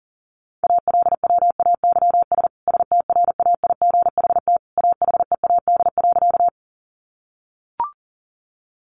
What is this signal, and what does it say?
Morse code, heard as a keyed tone, about 40 wpm, 700 Hz. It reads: ALWAYS STRAIGHT AHEAD.
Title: Sentences from Top 1100 Words 40wpm